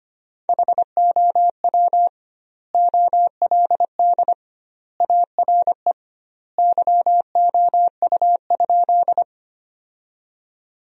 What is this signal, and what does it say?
Morse code, heard as a keyed tone, about 25 wpm, 700 Hz. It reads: HOW OLD ARE YOU?
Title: 2x Word-Spacing - Sentences from Top 200 Words - 25wpm